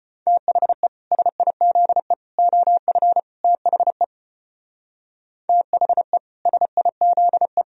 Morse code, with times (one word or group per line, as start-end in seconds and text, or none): 0.3-0.9 THE
1.1-2.1 SIZE
2.4-3.2 OF
3.4-4.0 THE
5.5-6.2 THE
6.5-7.6 SIZE